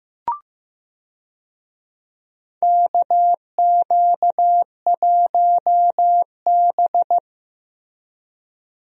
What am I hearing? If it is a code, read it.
Morse code, heard as a keyed tone, about 15 wpm, 700 Hz. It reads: KQ1B
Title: USA Call Signs - 4 Characters - Repeated 2x - 15wpm